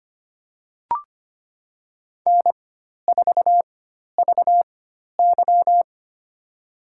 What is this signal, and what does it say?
Morse code, heard as a keyed tone, about 25 wpm, spaced as for 12 wpm, 700 Hz. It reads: N4VY
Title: USA Call Signs - 4x Character Spacing - 25wpm